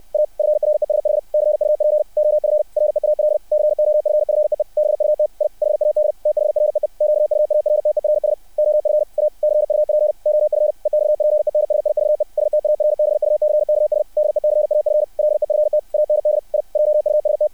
Morse code, heard as a keyed tone, about 60 wpm, 600 Hz. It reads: A FRESH CUP OF REAL COFFEE BUT I HAD THREE QUARTERS OF A CUP OF EXPENSIVE STARBUCKS DECAF LEFT AND I JUST
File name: archived / CRSnet